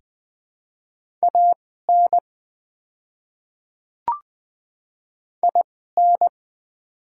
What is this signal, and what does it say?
Morse code, heard as a keyed tone, about 20 wpm, spaced as for 15 wpm, 700 Hz. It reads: AN IN
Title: 2x Character Spacing - Top 200 Words - 20wpm